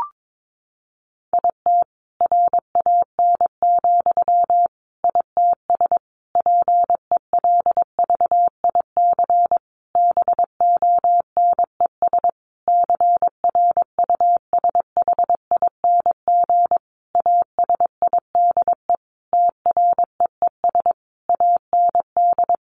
Morse code, heard as a keyed tone, about 22 wpm, 700 Hz. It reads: IT RAN, ITS PELVIC BONES CRUSHING ASIDE TREES AND
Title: A Sound Of Thunder 22wpm